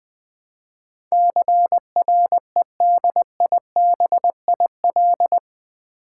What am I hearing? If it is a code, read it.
Morse code, heard as a keyed tone, about 20 wpm, 700 Hz. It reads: CREDIBIL
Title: Build A Word - 4 Letters At A Time - 11-Letter Words - 20wpm